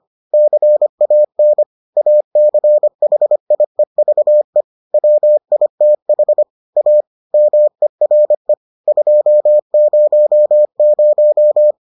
Morse code, heard as a keyed tone, about 25 wpm, 600 Hz. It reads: CAN ACHIEVE WITH A MERE 200